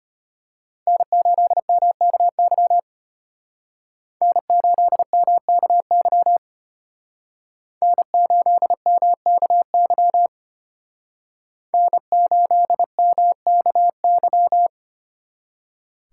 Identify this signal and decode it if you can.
Morse code, heard as a keyed tone, about 31 wpm, 700 Hz. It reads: N8MKY N8MKY N8MKY N8MKY
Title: Speed Racing - USA 1x3 Call Signs 25wpm